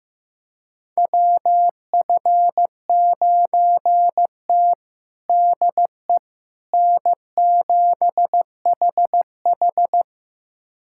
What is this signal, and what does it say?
Morse code, heard as a keyed tone, about 15 wpm, 700 Hz. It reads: WF9T DE N7HH
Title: USA Callsign DE Callsign - 4 Characters - 15wpm